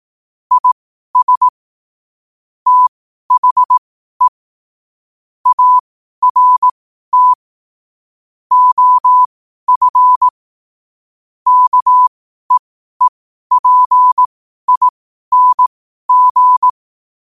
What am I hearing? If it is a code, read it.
Morse code, heard as a keyed tone, about 18 wpm, spaced as for 12 wpm, 1000 Hz. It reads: IS THE ART OF KEEPING
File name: qotdfiles / 12wpm